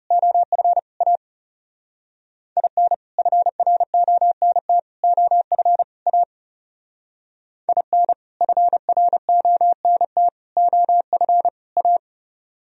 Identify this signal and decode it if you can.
Morse code, heard as a keyed tone, about 37 wpm, 700 Hz. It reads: OF A IN FRONT OF A IN FRONT OF A
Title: Speed-Racing - Sets of 4 Words 30wpm